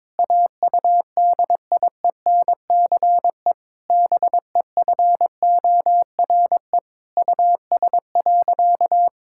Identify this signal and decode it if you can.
Morse code, heard as a keyed tone, about 22 wpm, 700 Hz. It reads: AUDIENCE BEFORE US.
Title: Sentences from Top 1000 Words - Review Part 5 - 22wpm